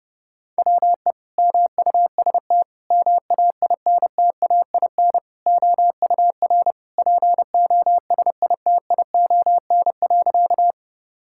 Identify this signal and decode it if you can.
Morse code, heard as a keyed tone, about 30 wpm, 700 Hz. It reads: WE MUST MAINTAIN OUR POSITION.
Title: Sentences from Top 1000 Words - Review Part 5 - 30wpm